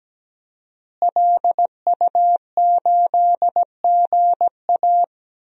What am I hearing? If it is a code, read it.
Morse code, heard as a keyed tone, about 17 wpm, 700 Hz. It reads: LU8GA
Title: DXCC-100 Call Signs - 5 Characters - Repeated 2x - 17wpm